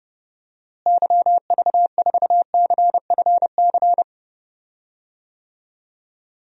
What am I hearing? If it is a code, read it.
Morse code, heard as a keyed tone, about 32 wpm, 700 Hz. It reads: YV4CFC
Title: DXCC-100 Call Signs - 6 Characters - Repeated 2x - 30wpm